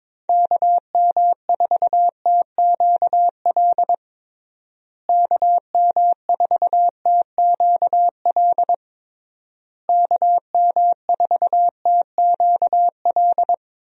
Morse code, heard as a keyed tone, about 22 wpm, 700 Hz. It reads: KM4TQL KM4TQL KM4TQL
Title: USA Call Signs - 6 Characters - Repeated 3x - 22wpm